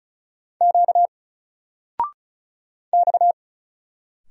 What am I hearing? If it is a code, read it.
Morse code, heard as a keyed tone, about 36 wpm, 700 Hz. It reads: Q X
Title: Single Letter-Number 35wpm